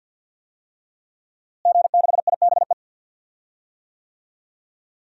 Morse code, heard as a keyed tone, about 46 wpm, 700 Hz. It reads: G6IBE
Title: DXCC-100 Call Signs - 5 Characters - Repeated 2x - 50wpm